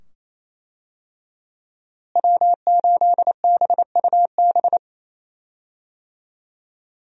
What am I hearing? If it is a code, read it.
Morse code, heard as a keyed tone, about 27 wpm, 700 Hz. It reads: W8BUB
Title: USA Call Signs - 5 Characters - Repeated 2x - 28wpm